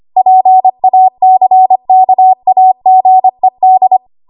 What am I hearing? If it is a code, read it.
Morse code, heard as a keyed tone, about 25 wpm, 750 Hz. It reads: PACKAGED